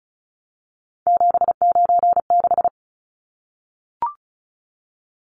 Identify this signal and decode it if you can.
Morse code, heard as a keyed tone, about 34 wpm, 700 Hz. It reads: 796
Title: Numbers - 3-Digit 35wpm